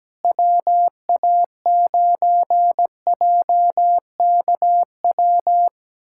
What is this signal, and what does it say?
Morse code, heard as a keyed tone, about 17 wpm, 700 Hz. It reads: WA9JKW
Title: USA Callsign DE Callsign - 6 Characters - 17wpm